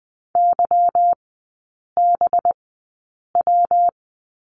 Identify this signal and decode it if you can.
Morse code, heard as a keyed tone, about 20 wpm, spaced as for 8 wpm, 700 Hz. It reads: YBW
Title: 184 Lesson 30e - Letter b - Three Characters 20wpm